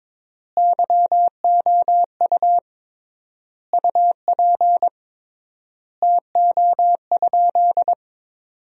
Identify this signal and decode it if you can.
Morse code, heard as a keyed tone, about 22 wpm, 700 Hz. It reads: YOU UP TO?
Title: 3x Word-Spacing - Sentences from Top 100 Words - 22wpm